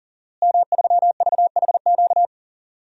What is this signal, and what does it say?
Morse code, heard as a keyed tone, about 40 wpm, 700 Hz. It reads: M3VHQ
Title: DXCC-100 Call Signs - 5 Characters - Repeated 2x - 40wpm